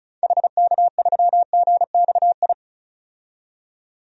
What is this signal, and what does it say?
Morse code, heard as a keyed tone, about 34 wpm, 700 Hz. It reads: HK3GXI
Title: DXCC-100 Call Signs - 6 Characters - 35wpm